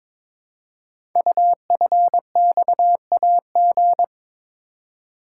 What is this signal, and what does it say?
Morse code, heard as a keyed tone, about 22 wpm, 700 Hz. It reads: UFXAG
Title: Code Groups - 5 Letters - 22wpm